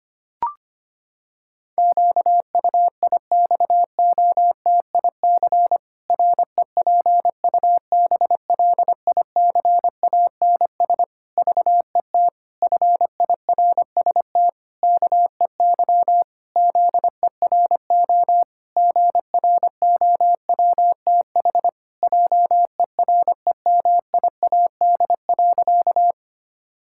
Morse code, heard as a keyed tone, about 25 wpm, 700 Hz. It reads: QUIXOTIC REPUBLICANS VET FIRST KEY ZERO GROWTH JEREMIAD.
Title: Pangram Challenge v2 - 25wpm